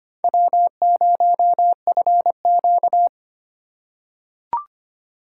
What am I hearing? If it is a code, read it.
Morse code, heard as a keyed tone, about 25 wpm, 700 Hz. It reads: W0FQ
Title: USA Call Signs - 4x Character Spacing - 25wpm